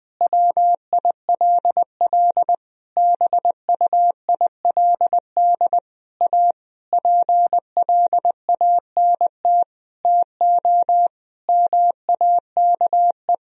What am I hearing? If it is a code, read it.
Morse code, heard as a keyed tone, about 20 wpm, 700 Hz. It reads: WILL BUILD A PLANT TO MAKE
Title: Sentences from Top 1000 Words - Review Part 1 - 20wpm